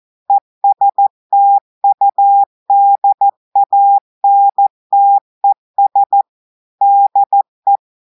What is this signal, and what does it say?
Morse code, heard as a keyed tone, about 14 wpm, 800 Hz. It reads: ESTUDANTES DE
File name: audios / portugais / portugais texte en clair 1 vitesse 14 mots minutes